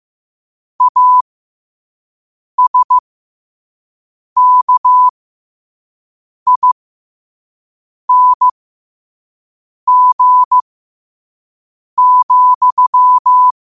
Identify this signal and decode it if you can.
Morse code, heard as a keyed tone, about 15 wpm, spaced as for 5 wpm, 1000 Hz. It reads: ASKING,
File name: 5wpm